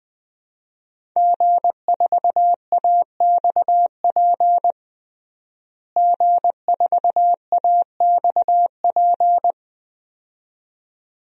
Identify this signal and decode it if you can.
Morse code, heard as a keyed tone, about 20 wpm, 700 Hz. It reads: G4AXP G4AXP
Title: DXCC-100 Call Signs - 5 Characters - Repeated 2x - 20wpm